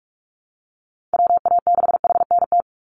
Morse code, heard as a keyed tone, about 44 wpm, 700 Hz. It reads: WA6HNT
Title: USA Call Signs 45wpm